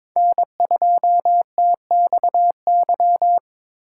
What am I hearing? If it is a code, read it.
Morse code, heard as a keyed tone, about 26 wpm, 700 Hz. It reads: N2TXY
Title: Speed Racing - USA 1x3 Call Signs 22wpm